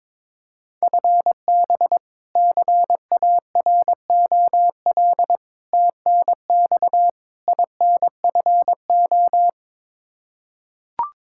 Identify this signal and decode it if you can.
Morse code, heard as a keyed tone, about 22 wpm, 700 Hz. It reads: FB CAROL TNX INFO
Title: Practice QSO - Exchange 3 - More Info - 22wpm